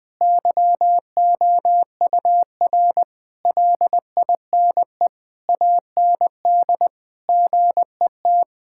Morse code, heard as a keyed tone, about 20 wpm, 700 Hz. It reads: YOUR LINE AND GET
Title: Sentences from Top 1000 Words - Review Part 6 - 20wpm